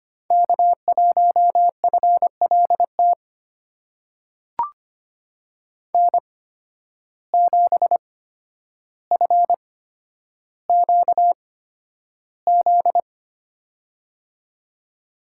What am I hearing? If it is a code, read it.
Morse code, heard as a keyed tone, about 25 wpm, spaced as for 11 wpm, 700 Hz. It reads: K1FLT N7FQZ
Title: USA Call Signs - 5 Characters - 8x Character Spacing - 25wpm